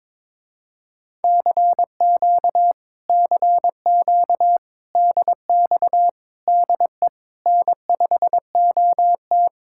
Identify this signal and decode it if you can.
Morse code, heard as a keyed tone, about 22 wpm, 700 Hz. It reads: CQ CQ DX DE N5OT
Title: Speed-Racing Common QSO Phrases - 22wpm